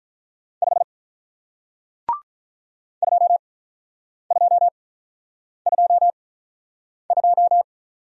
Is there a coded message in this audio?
Yes: Morse code from 0.6 to 7.6 s, heard as a keyed tone, about 48 wpm, 700 Hz. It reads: 5 2 2 2 2